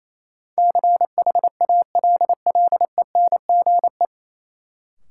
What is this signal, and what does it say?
Morse code, heard as a keyed tone, about 28 wpm, 700 Hz. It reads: CHALLENGE
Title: Sentences from Top 1000 Words - Review Part 7 - New Word Drill - 28wpm